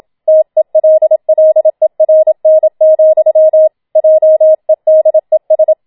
Morse code, heard as a keyed tone, about 27 wpm, 630 Hz. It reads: TELLERN, JEDES
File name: Schneewittchen Teil2 27wpm